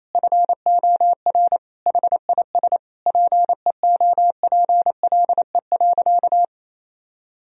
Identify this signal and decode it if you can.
Morse code, heard as a keyed tone, about 28 wpm, 700 Hz. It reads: FOR HIS PEOPLE.
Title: Sentences from Top 1000 Words - Review Part 3 - 28wpm